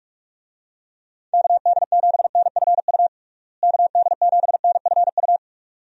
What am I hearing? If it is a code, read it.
Morse code, heard as a keyed tone, about 41 wpm, 700 Hz. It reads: KD7NFU KD7NFU
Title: USA Call Signs - 6 Characters - Repeated 2x - 45wpm